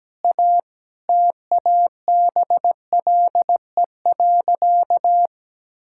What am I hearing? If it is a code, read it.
Morse code, heard as a keyed tone, about 17 wpm, 700 Hz. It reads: A TABLE.
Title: Sentences from Top 1600 Words 17wpm